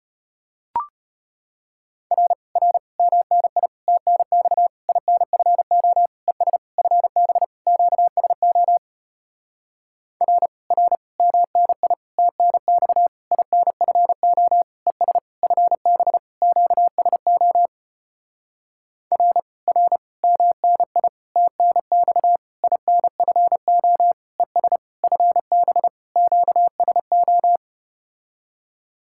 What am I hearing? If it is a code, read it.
Morse code, heard as a keyed tone, about 33 wpm, 700 Hz. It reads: R R MNI TNX INFO ES FB QSO R R MNI TNX INFO ES FB QSO R R MNI TNX INFO ES FB QSO